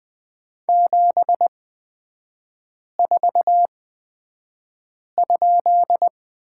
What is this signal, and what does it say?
Morse code, heard as a keyed tone, about 20 wpm, spaced as for 5 wpm, 700 Hz. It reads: 74?
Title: 236 Lesson 38e - Letter q - Three Characters 20wpm